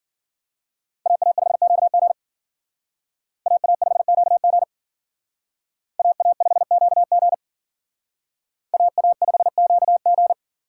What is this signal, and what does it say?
Morse code, heard as a keyed tone, about 52 wpm, 700 Hz. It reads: AA5QG AA5QG AA5QG AA5QG